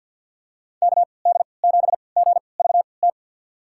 Morse code, heard as a keyed tone, about 48 wpm, spaced as for 28 wpm, 700 Hz. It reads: KD7GVT
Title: USA Call Signs - 6 Characters - 3x Character Spacing - 50wpm